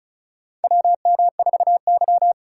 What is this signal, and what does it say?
Morse code, heard as a keyed tone, about 36 wpm, 700 Hz. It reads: WM4Y
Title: USA Call Signs - 4 Characters - 4x Character Spacing - 35wpm